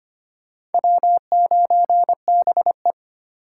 Morse code, heard as a keyed tone, about 33 wpm, 700 Hz. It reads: W9BE